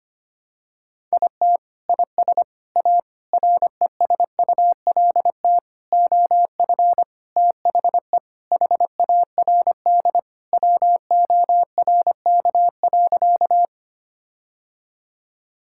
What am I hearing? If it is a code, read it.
Morse code, heard as a keyed tone, about 25 wpm, 700 Hz. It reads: IT IS A RESULT OF THE HARD WORK.
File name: practice / 25wpm / Sentences with 4-Word Phrases - 25wpm